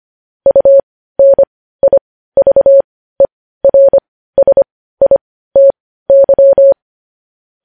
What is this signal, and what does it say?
Morse code, heard as a keyed tone, about 25 wpm, spaced as for 15 wpm, 550 Hz. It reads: UNIVERSITY
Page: News Headlines in Morse Code at 15 WPM